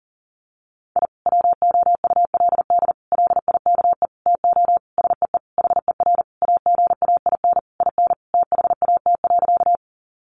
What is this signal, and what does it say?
Morse code, heard as a keyed tone, about 40 wpm, 700 Hz. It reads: I WOULD LIKE TO SEE HER AGAIN IN THAT.
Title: Sentences from Top 200 Words 40wpm